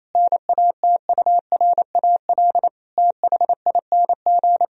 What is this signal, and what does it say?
Morse code, heard as a keyed tone, about 27 wpm, 700 Hz. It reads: NATURAL THING
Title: Sentences from Top 1000 Words - Review Part 4 - 28wpm